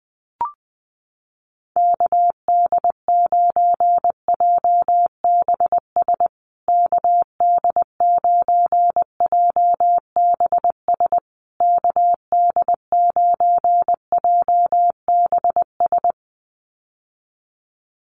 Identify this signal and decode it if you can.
Morse code, heard as a keyed tone, about 20 wpm, 700 Hz. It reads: KD9JBS KD9JBS KD9JBS
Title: USA Call Signs Repeated 3x 20wpm